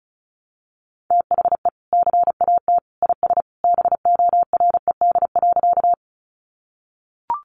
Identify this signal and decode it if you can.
Morse code, heard as a keyed tone, about 34 wpm, 700 Hz. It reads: THE CAT IS BORED.
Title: Easy Phrases 35wpm